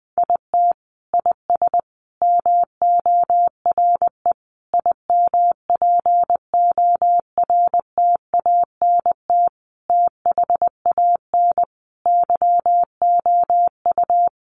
Morse code, heard as a keyed tone, about 20 wpm, 700 Hz. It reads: IT IS MORE IMPORTANT THAN YOU
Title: Sentences from Top 600 Words 20wpm